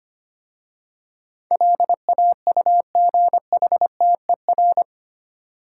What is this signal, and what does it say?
Morse code, heard as a keyed tone, about 25 wpm, 700 Hz. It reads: LAUGHTER
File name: 8 Letter Words - First 3 Letters Given - 25wpm